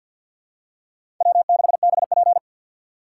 Morse code, heard as a keyed tone, about 47 wpm, 700 Hz. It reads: W6BP
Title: USA Call Signs - 4 Characters - 3x Character Spacing - 50wpm